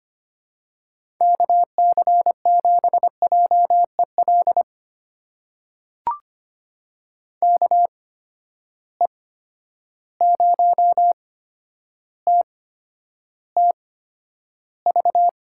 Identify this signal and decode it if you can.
Morse code, heard as a keyed tone, about 25 wpm, spaced as for 11 wpm, 700 Hz. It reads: KC7JEL KE0TTV
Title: USA Call Signs - 6 Characters - 8x Character Spacing - 25wpm